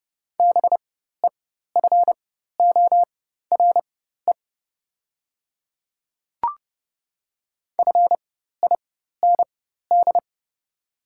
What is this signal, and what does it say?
Morse code, heard as a keyed tone, about 30 wpm, spaced as for 14 wpm, 690 Hz. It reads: BEFORE FIND
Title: 4x Character Spacing - Top 500 Words - 30wpm